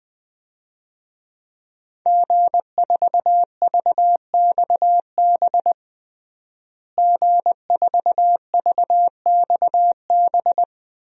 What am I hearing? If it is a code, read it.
Morse code, heard as a keyed tone, about 20 wpm, 700 Hz. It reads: G4VXB G4VXB
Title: DXCC-100 Call Signs - 5 Characters - Repeated 2x - 20wpm